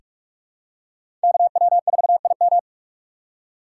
Morse code, heard as a keyed tone, about 44 wpm, 700 Hz. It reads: KW4IM
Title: USA Call Signs - 3x Character Spacing - 45wpm